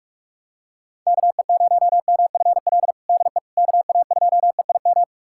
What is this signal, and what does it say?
Morse code, heard as a keyed tone, about 46 wpm, 700 Hz. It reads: KE0MFL DE KA2EIM
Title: USA Callsign DE Callsign - 6 Characters - 45wpm